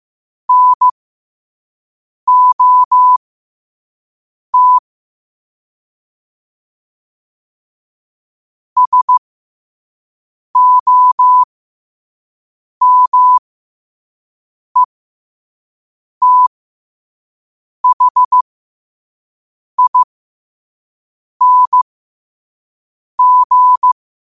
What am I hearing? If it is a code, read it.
Morse code, heard as a keyed tone, about 15 wpm, spaced as for 5 wpm, 1000 Hz. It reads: NOT SOMETHING
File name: 5wpm